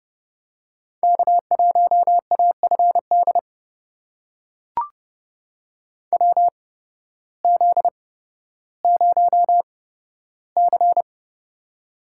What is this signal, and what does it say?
Morse code, heard as a keyed tone, about 30 wpm, spaced as for 14 wpm, 700 Hz. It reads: K1AFD WZ0C
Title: USA Call Signs - 8x Character Spacing - 30wpm